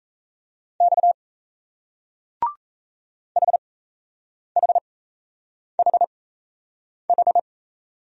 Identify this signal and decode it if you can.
Morse code, heard as a keyed tone, about 35 wpm, spaced as for 8 wpm, 700 Hz. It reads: X HHHH